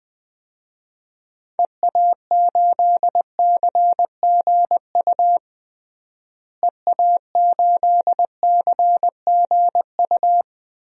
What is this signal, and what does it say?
Morse code, heard as a keyed tone, about 19 wpm, 700 Hz. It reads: EA8CGU EA8CGU
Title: DXCC-100 Call Signs - 6 Characters - Repeated 2x - 20wpm